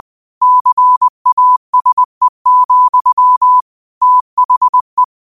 Morse code, heard as a keyed tone, about 20 wpm, 1000 Hz. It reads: CASE, THE
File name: qotdfiles / 20wpm